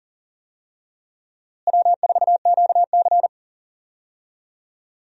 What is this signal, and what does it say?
Morse code, heard as a keyed tone, about 40 wpm, 700 Hz. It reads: W4QC
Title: USA Call Signs - 4 Characters - Repeated 2x - 40wpm